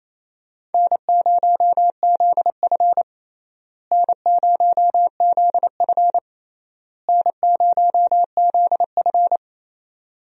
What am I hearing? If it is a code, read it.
Morse code, heard as a keyed tone, about 28 wpm, 700 Hz. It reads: N0ZF N0ZF N0ZF
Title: USA Call Signs - 4 Characters - Repeated 3x - 28wpm